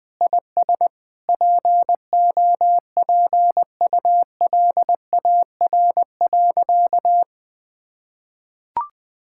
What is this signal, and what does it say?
Morse code, heard as a keyed tone, about 20 wpm, 700 Hz. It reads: IS POPULAR.
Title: Sentences with 3-Word Phrases - Spoken Beginning - 20wpm